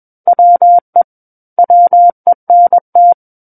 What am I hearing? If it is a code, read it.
Morse code, heard as a keyed tone, about 21 wpm, 700 Hz. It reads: WE WENT
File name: Book One / 12-Chapter 12 - 21 WPM